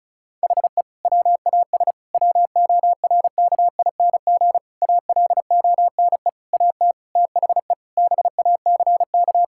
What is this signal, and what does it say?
Morse code, heard as a keyed tone, about 34 wpm, 700 Hz. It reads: HE WAS WORKING ALONE AT THE BACK
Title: Sentences from Top 1000 Words - Review Part 5 - 35wpm